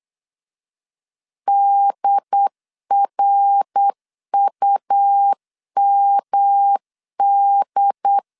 Morse code, heard as a keyed tone, about 8 wpm, 800 Hz. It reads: DRUMD
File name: CW / DET / 1 3 1500C S C 8wpm 4214